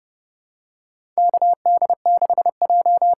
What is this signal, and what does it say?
Morse code, heard as a keyed tone, about 30 wpm, 700 Hz. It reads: KD6J
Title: USA Call Signs - 6 Characters - 8x Character Spacing - 30wpm